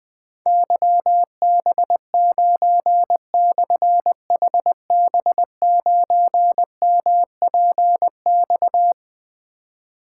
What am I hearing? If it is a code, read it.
Morse code, heard as a keyed tone, about 20 wpm, 700 Hz. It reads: YB9/HB9MPX
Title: Difficult Call Signs 20wpm